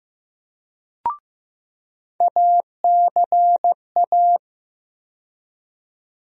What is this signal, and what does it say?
Morse code, heard as a keyed tone, about 15 wpm, 700 Hz. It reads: ACA
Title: USA Call Sign Postfixes - 3 Letters 15wpm